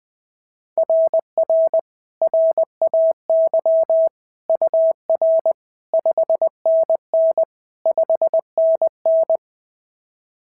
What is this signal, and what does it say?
Morse code, heard as a keyed tone, about 20 wpm, 690 Hz. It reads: RR RAY UR 5NN 5NN
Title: SOTA QSO Practice 20wpm